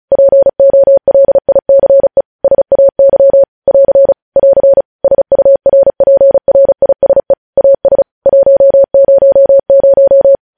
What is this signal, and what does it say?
Morse code, heard as a keyed tone, about 36 wpm, 550 Hz. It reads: POLICE SAY <AR> <AR> SURPRISE AS 100
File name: podcast / News headlines-1x-35x35